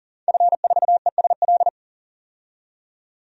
Morse code, heard as a keyed tone, about 40 wpm, 700 Hz. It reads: F4ESL
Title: DXCC-100 Call Signs - 5 Characters - 40wpm